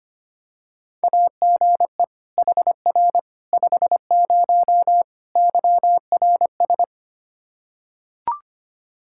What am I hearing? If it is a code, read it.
Morse code, heard as a keyed tone, about 25 wpm, 700 Hz. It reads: AGE HR 50 YRS
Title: K1USN SST QSO Practice 25wpm